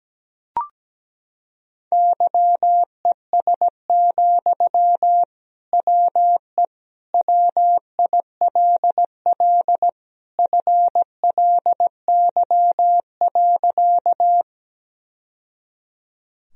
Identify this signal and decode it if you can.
Morse code, heard as a keyed tone, about 17 wpm, 700 Hz. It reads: YES, WE WILL FLY.
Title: Sentences from Top 1000 Words - Review Part 2 - 17wpm